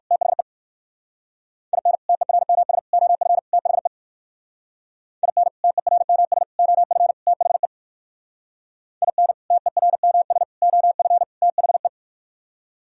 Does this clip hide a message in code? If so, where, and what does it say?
Yes: Morse code from 0.1 to 11.9 s, heard as a keyed tone, about 57 wpm, 700 Hz. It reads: THE IN TERMS OF THE IN TERMS OF THE IN TERMS OF THE